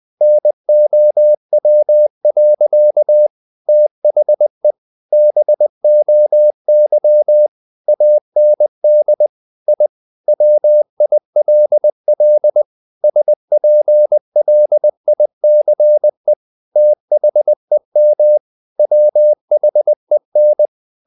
Morse code, heard as a keyed tone, about 20 wpm, 600 Hz. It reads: NOW. THE BOY AND I WILL SPLICE THEM WHEN
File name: the old man and the sea part 2 20wpm